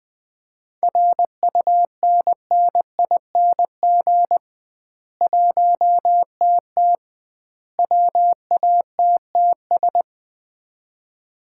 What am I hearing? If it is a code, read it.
Morse code, heard as a keyed tone, about 20 wpm, 700 Hz. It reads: RUNNING 1TT WATTS
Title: Common QSO Phrases - 2x Word-Spacing - 20wpm